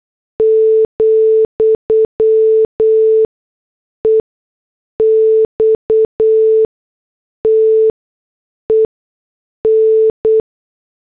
Morse code, heard as a keyed tone, about 8 wpm, spaced as for 6 wpm, 440 Hz. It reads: , EXTEN
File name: morse / headlines